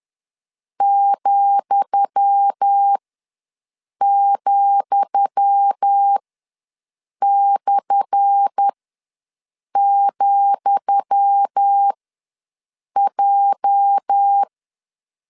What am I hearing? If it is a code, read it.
Morse code, heard as a keyed tone, about 10 wpm, spaced as for 6 wpm, 800 Hz. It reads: ,,/,J
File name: LEZIONI / 10 SIGNS / 10D COMA ws 10wpm 4314